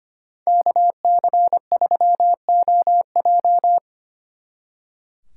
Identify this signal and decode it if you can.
Morse code, heard as a keyed tone, about 31 wpm, 700 Hz. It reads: KC3OJ
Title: Speed Racing - USA 2x2 Call Signs 25wpm